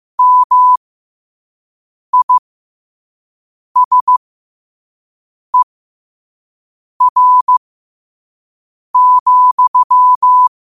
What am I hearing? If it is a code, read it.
Morse code, heard as a keyed tone, about 15 wpm, spaced as for 5 wpm, 1000 Hz. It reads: MISER,
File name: qotdfiles / 5wpm